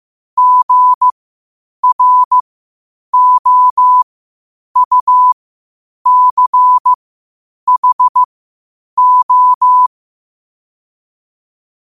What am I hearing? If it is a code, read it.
Morse code, heard as a keyed tone, about 15 wpm, spaced as for 8 wpm, 1000 Hz. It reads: GROUCHO
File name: qotdfiles / 8wpm